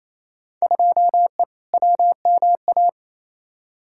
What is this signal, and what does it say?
Morse code, heard as a keyed tone, about 28 wpm, 700 Hz. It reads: 2E WMA
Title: Field Day Exchanges Encore - 28wpm